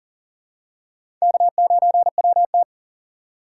Morse code, heard as a keyed tone, about 40 wpm, 700 Hz. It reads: K9WT
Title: USA Call Signs - 3x Character Spacing - 40wpm